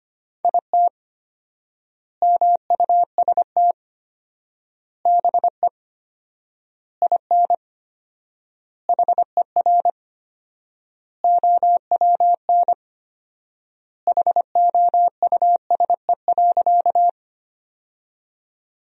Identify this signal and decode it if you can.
Morse code, heard as a keyed tone, about 25 wpm, 700 Hz. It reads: IT MUST BE IN HER OWN HOUSE.
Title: 4x Word-Spacing - Sentences from Top 300 Words - 25wpm